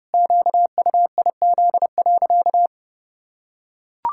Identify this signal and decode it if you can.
Morse code, heard as a keyed tone, about 30 wpm, 700 Hz. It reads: QUIZ.
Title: Pangram Challenge v2 - 30wpm